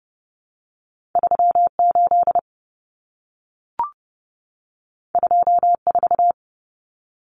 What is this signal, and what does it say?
Morse code, heard as a keyed tone, about 30 wpm, 700 Hz. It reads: 38 24
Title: Numbers - 2-Digit 30wpm